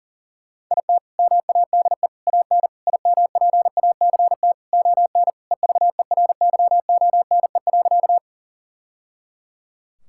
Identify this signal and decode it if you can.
Morse code, heard as a keyed tone, about 40 wpm, 700 Hz. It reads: IT MADE AN IMPACT ON EVERYONE.
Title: Sentences from Top 1000 Words - Review Part 6 - 40wpm